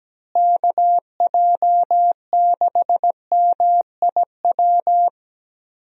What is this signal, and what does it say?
Morse code, heard as a keyed tone, about 17 wpm, 700 Hz. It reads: KJ6MIW
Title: USA Call Signs - 6 Characters - Repeated 2x - 17wpm